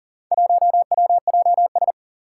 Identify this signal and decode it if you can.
Morse code, heard as a keyed tone, about 40 wpm, 690 Hz. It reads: 1WJS
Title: Four Challenging Letter-Number Combinations 40wpm